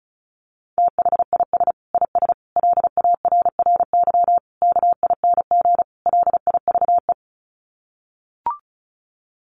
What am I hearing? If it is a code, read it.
Morse code, heard as a keyed tone, about 34 wpm, 700 Hz. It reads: THIS IS LARRY KING LIVE
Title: Sets of 5 Words Encore2 35wpm